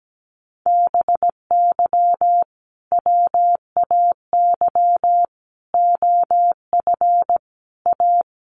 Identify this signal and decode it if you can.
Morse code, heard as a keyed tone, about 17 wpm, 700 Hz. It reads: BY WAY OF A
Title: Sentences from Top 1500 Words 17wpm